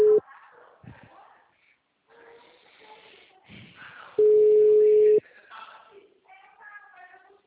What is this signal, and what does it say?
Ringback tone (Europe, 425 Hz), with speech.